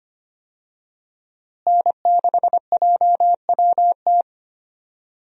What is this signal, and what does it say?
Morse code, heard as a keyed tone, about 25 wpm, 700 Hz. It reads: N6JWT
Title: USA Call Signs - 5 Characters - Repeated 2x - 25wpm